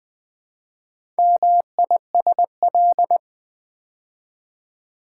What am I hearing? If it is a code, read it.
Morse code, heard as a keyed tone, about 20 wpm, 700 Hz. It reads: MISL
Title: Build A Word - 4 Letters At A Time - 10-Letter Words - 20wpm